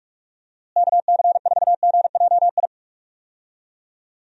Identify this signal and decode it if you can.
Morse code, heard as a keyed tone, about 41 wpm, 700 Hz. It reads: KC4GJI
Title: USA Call Signs - 6 Characters - Repeated 2x - 45wpm